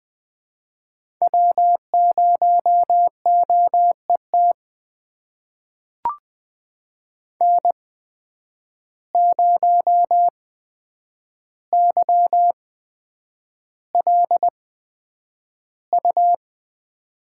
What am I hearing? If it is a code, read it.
Morse code, heard as a keyed tone, about 20 wpm, spaced as for 9 wpm, 700 Hz. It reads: W0OET N0YLU
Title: USA Call Signs - 5 Characters - 8x Character Spacing - 20wpm